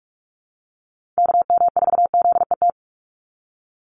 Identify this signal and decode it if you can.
Morse code, heard as a keyed tone, about 46 wpm, 700 Hz. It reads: KM4ZET
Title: USA Call Signs 45wpm